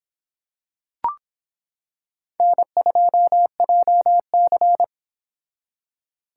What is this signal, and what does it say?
Morse code, heard as a keyed tone, about 25 wpm, 700 Hz. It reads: N2JC